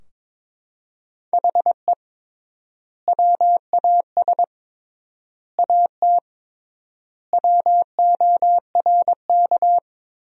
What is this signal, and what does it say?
Morse code, heard as a keyed tone, about 22 wpm, 700 Hz. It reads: HE WAS AT WORK
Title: 3x Word-Spacing - Sentences from Top 200 Words - 22wpm